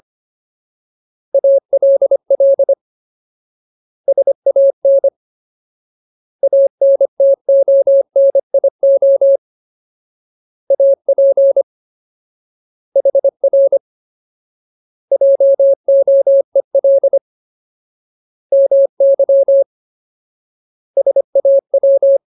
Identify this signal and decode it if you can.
Morse code, heard as a keyed tone, about 25 wpm, 550 Hz. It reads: ALL SAN ANTONIO AP HR JOEL MY SAW
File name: Ultimate Word List / 25WPM Ultimate